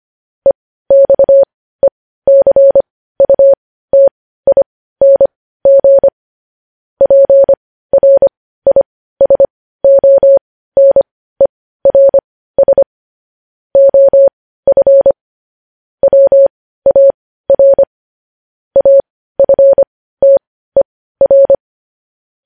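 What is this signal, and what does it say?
Morse code, heard as a keyed tone, about 25 wpm, spaced as for 15 wpm, 550 Hz. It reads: EXECUTING PRISONERS OF WAR AFTER
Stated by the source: Morse code news headlines sent at 15 words per minute.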